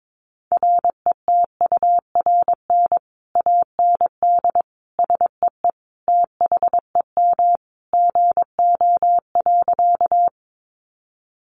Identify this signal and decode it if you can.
Morse code, heard as a keyed tone, about 22 wpm, 700 Hz. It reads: RETURN AND SEE THEM GO.
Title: Sentences from Top 800 Words 22wpm